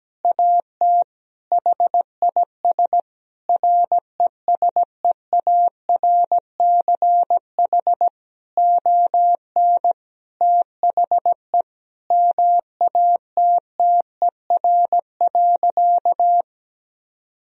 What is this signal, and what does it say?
Morse code, heard as a keyed tone, about 17 wpm, 700 Hz. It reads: AT HIS RESEARCH ON THE MATTER.
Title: Sentences from Top 1000 Words - Review Part 4 - 17wpm